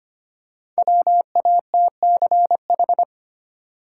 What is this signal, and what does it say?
Morse code, heard as a keyed tone, about 25 wpm, 700 Hz. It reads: WATCH
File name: practice / 25wpm / Easy Words with Hints - 25wpm